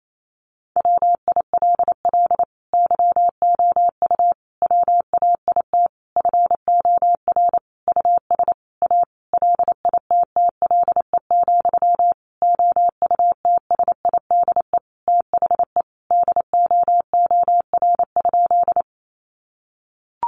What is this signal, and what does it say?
Morse code, heard as a keyed tone, about 28 wpm, 690 Hz. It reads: WILL YOU WAIT FOR US A LITTLE, OUTSIDE THE DOOR?
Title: Sentences from Top 900 Words 28wpm